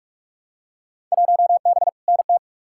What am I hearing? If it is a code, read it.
Morse code, heard as a keyed tone, about 44 wpm, 700 Hz. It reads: 1B NT
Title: Field Day Exchanges Encore - 45wpm